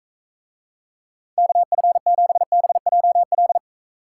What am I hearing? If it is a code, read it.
Morse code, heard as a keyed tone, about 40 wpm, 700 Hz. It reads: KF7BJL